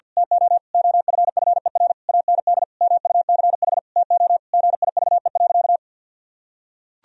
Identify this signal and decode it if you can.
Morse code, heard as a keyed tone, about 50 wpm, 700 Hz. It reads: TO OFFER AND MUCH TO GIVE.
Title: Sentences from Top 1000 Words - Review Part 4 - 50wpm